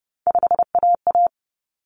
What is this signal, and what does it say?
Morse code, heard as a keyed tone, about 30 wpm, 700 Hz. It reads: 5AA
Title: Common Cut Numbers - 3-Digit 30wpm